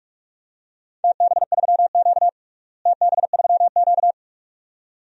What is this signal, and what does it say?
Morse code, heard as a keyed tone, about 46 wpm, 700 Hz. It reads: TB3Q TB3Q
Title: DXCC-100 Call Signs - 4 Characters - Repeated 2x - 45wpm